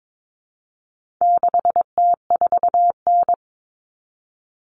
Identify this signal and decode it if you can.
Morse code, heard as a keyed tone, about 22 wpm, 700 Hz. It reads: 6T4N
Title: Common Cut Numbers - 4-Digit 22wpm